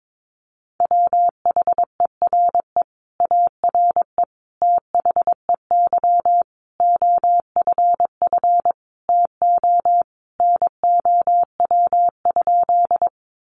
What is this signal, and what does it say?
Morse code, heard as a keyed tone, about 22 wpm, 700 Hz. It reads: WHERE ARE THEY OFF TO NOW?
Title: Sentences from Top 200 Words 22wpm